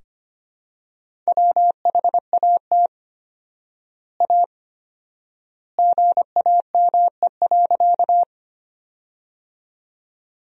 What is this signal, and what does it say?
Morse code, heard as a keyed tone, about 25 wpm, 700 Hz. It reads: WHAT A GAME.
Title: 4x Word-Spacing - Sentences from Top 500 Words - 25wpm